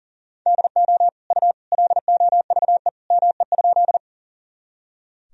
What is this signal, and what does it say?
Morse code, heard as a keyed tone, about 37 wpm, 700 Hz. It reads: DO U LOVE ME?
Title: Sentences from Top 400 Words and Abbreviations 40wpm